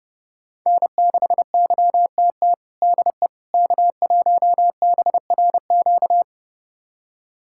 Morse code, heard as a keyed tone, about 32 wpm, 700 Hz. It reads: N6YTT DE K1BRQ
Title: USA Callsign DE Callsign 30wpm